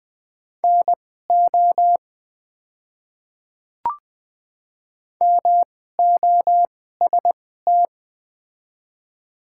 Morse code, heard as a keyed tone, about 20 wpm, spaced as for 14 wpm, 690 Hz. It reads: NO MOST
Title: 2x Character Spacing - Top 100 Words - 20wpm